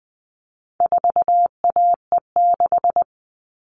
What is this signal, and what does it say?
Morse code, heard as a keyed tone, about 20 wpm, 700 Hz. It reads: 4AE6
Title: All Cut Numbers - 4-Digit 20wpm